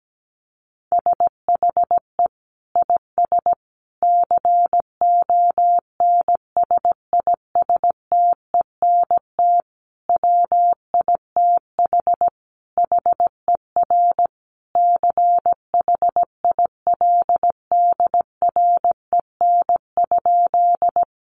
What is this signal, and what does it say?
Morse code, heard as a keyed tone, about 17 wpm, 700 Hz. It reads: SHE IS CONSISTENT WITH HER CHILDREN?
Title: Sentences from Top 1500 Words 17wpm